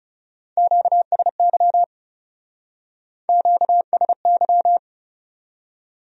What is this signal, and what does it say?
Morse code, heard as a keyed tone, about 37 wpm, 700 Hz. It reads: QSY QSY
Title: Speed-Racing Q-Codes - 30wpm